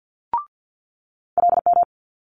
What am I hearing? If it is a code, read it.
Morse code, heard as a keyed tone, about 50 wpm, 700 Hz. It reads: LM
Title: Multi-letter Phonemes Quick Fire 50wpm